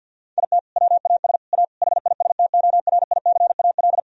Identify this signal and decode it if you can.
Morse code, heard as a keyed tone, about 50 wpm, 700 Hz. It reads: IT WAS A HISTORICAL
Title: Sentences from Top 1000 Words - Review Part 4 - 50wpm